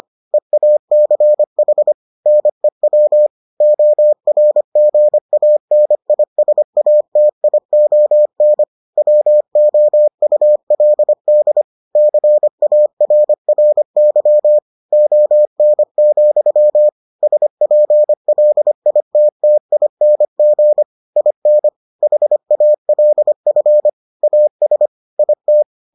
Morse code, heard as a keyed tone, about 25 wpm, 600 Hz. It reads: EACH NEW ORGANISATION WOULD CARRY ON, SPLITTING IN HALF AS IT